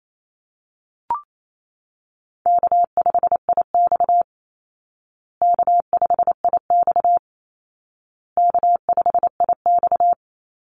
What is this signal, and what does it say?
Morse code, heard as a keyed tone, about 28 wpm, 700 Hz. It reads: K5IX K5IX K5IX
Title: Call Signs Repeated 2x - 28wpm